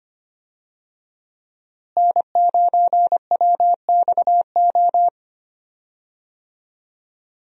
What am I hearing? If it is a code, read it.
Morse code, heard as a keyed tone, about 25 wpm, 700 Hz. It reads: N9WXO
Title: USA Call Signs - 5 Characters - Repeated 3x - 25wpm